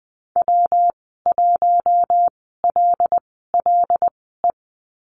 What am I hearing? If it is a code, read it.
Morse code, heard as a keyed tone, about 20 wpm, spaced as for 15 wpm, 700 Hz. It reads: W1LLE
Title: 173 Lesson 28g - Number 9 - Call Signs 20wpm